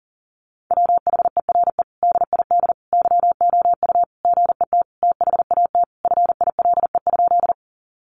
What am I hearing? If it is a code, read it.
Morse code, heard as a keyed tone, about 37 wpm, 700 Hz. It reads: WHERE DID YOU GET THAT FILE?
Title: Sentences from Top 1200 Words 40wpm